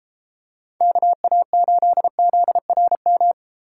Morse code, heard as a keyed tone, about 32 wpm, 700 Hz. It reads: KA8ZRM